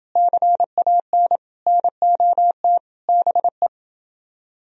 Morse code, heard as a keyed tone, about 28 wpm, 700 Hz. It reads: CAN NOT BE